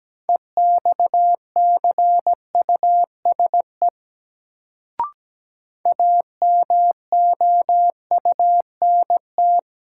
Morse code, heard as a keyed tone, about 17 wpm, 700 Hz. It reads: EXCUSE AMOUNT
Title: Sentences from Top 700 Words - New Word Drill - 17wpm